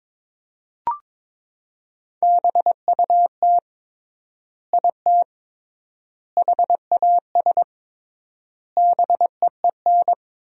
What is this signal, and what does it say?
Morse code, heard as a keyed tone, about 21 wpm, 700 Hz. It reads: BUT IT HAS BEEN
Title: 3x Word-Spacing - Sentences from Top 100 Words - 22wpm